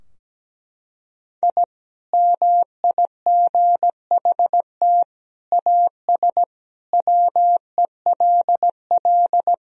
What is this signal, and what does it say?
Morse code, heard as a keyed tone, about 17 wpm, 700 Hz. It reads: I MIGHT AS WELL
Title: Sentences from Top 1000 Words - Review Part 2 - 17wpm